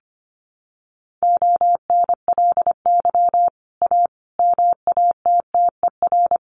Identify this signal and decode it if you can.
Morse code, heard as a keyed tone, about 25 wpm, 700 Hz. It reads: ONLY A MATTER
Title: Sets of 5 Words Encore 25wpm